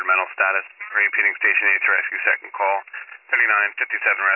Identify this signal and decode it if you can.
Emergency Alert System (SAME) tones, heard as harsh buzzing bursts, with speech. It is heard from 0.6 to 3.8 s.